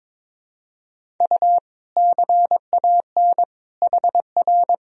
Morse code, heard as a keyed tone, about 22 wpm, 700 Hz. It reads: U CAN HR
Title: Sentences from Top 500 Words and Abbreviations 22wpm